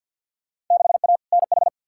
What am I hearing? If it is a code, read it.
Morse code, heard as a keyed tone, about 48 wpm, 700 Hz. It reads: 6A NH
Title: Field Day Exchanges - 50wpm